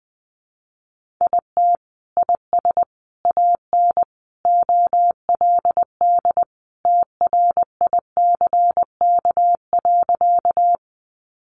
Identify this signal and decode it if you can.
Morse code, heard as a keyed tone, about 19 wpm, 700 Hz. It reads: IT IS AN OLD TRICK.
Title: Sentences from Top 1500 Words 20wpm